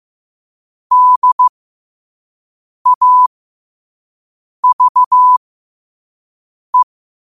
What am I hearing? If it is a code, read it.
Morse code, heard as a keyed tone, about 15 wpm, spaced as for 5 wpm, 1000 Hz. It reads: DAVE
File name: qotdfiles / 5wpm